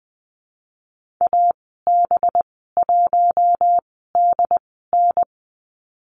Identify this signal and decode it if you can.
Morse code, heard as a keyed tone, about 20 wpm, spaced as for 15 wpm, 700 Hz. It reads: AB1DN
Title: 213 Lesson 34g - Number 8 - Call Signs 20wpm